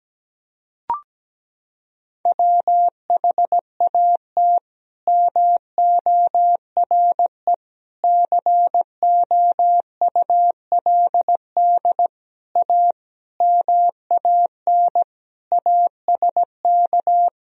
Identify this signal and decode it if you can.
Morse code, heard as a keyed tone, about 17 wpm, 700 Hz. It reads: WHAT MORE COULD A MAN ASK
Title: Sentences from Top 200 Words and Abbreviations 17wpm